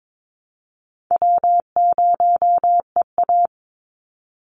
Morse code, heard as a keyed tone, about 22 wpm, 690 Hz. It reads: W0EA
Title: Easy Call Signs Repeated 2x - 22wpm